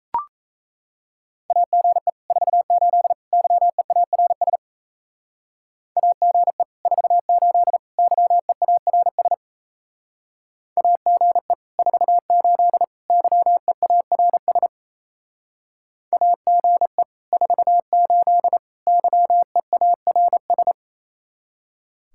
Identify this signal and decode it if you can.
Morse code, heard as a keyed tone, about 34 wpm, 700 Hz. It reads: AGE 48 YEARS AGE 48 YEARS AGE 48 YEARS AGE 48 YEARS
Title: Speed-Racing Common QSO Phrases - 28wpm